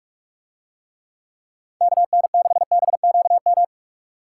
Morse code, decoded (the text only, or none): KN6BQM